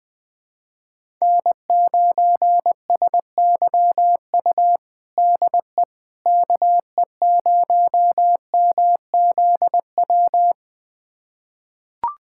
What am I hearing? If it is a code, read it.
Morse code, heard as a keyed tone, about 21 wpm, 700 Hz. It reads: N9SYU DE KE0MZW
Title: USA Callsign DE Callsign 20wpm